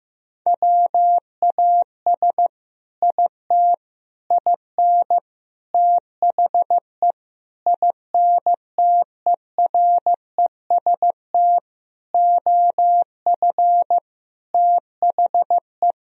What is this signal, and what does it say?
Morse code, heard as a keyed tone, about 16 wpm, 700 Hz. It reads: WAS IT IN THE INTEREST OF THE
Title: Sentences from Top 1000 Words - Review Part 4 - 15wpm